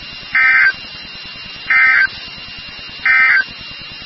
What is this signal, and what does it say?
Emergency Alert System (SAME) tones, heard as harsh buzzing bursts. It is heard from 0.2 to 3.5 s.
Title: NOAA Weather Radio WXK36